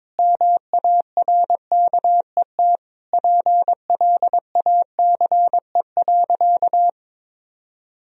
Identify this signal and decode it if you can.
Morse code, heard as a keyed tone, about 22 wpm, 700 Hz. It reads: MARKET PLACE.
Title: Sentences from Top 1000 Words - Review Part 3 - 22wpm